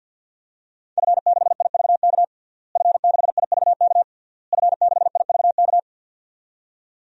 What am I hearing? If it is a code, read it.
Morse code, heard as a keyed tone, about 52 wpm, 700 Hz. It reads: F6IVK F6IVK F6IVK
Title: DXCC-100 Call Signs - 5 Characters - Repeated 3x - 50wpm